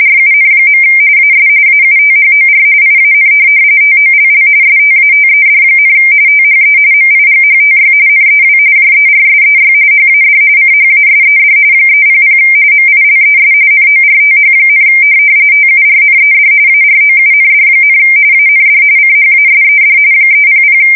RTTY, about 45.45 baud. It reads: RUN. AS AUTHOR IAN TOLL STATED, THE MIGHTY NEW JERSEY HAD CHASED 300 MILES NORTH, TURNED AROUND, AND CHASED 300
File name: fsk170 / USS-New-Jersey